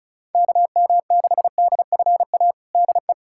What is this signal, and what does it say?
Morse code, heard as a keyed tone, about 36 wpm, 700 Hz. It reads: KM6DFA DE
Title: USA Callsign DE Callsign - 6 Characters - 35wpm